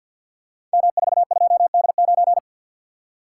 Morse code, heard as a keyed tone, about 47 wpm, 690 Hz. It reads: M42D9
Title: Code Groups - 5 Letter:Numbers - 50wpm